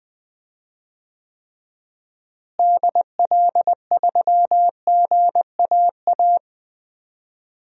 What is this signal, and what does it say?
Morse code, heard as a keyed tone, about 21 wpm, 700 Hz. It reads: DL3GAA
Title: DXCC-100 Call Signs - 6 Characters - Repeated 3x - 20wpm